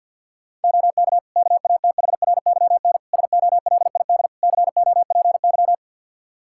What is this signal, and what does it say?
Morse code, heard as a keyed tone, about 47 wpm, 700 Hz. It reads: OK KATHRYN SOLID COPY
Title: K1USN SST QSO Practice 50wpm